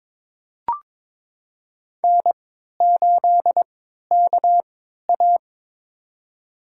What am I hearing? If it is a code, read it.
Morse code, heard as a keyed tone, about 22 wpm, spaced as for 13 wpm, 700 Hz. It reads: N8KA
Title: USA Call Signs - 4 Characters - 3x Character Spacing - 22wpm